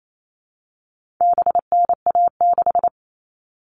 Morse code, heard as a keyed tone, about 28 wpm, 700 Hz. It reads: BNA6
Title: All Cut Numbers - 4-Digit 28wpm